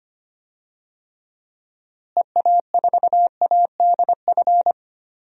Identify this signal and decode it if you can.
Morse code, heard as a keyed tone, about 25 wpm, 700 Hz. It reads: EA4ADF
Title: DXCC-100 Call Signs - 6 Characters - Repeated 2x - 25wpm